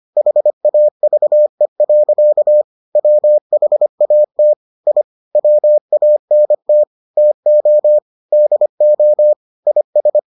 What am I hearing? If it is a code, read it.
Morse code, heard as a keyed tone, about 25 wpm, 600 Hz. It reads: HAVE. WHAT I WANT TO DO IS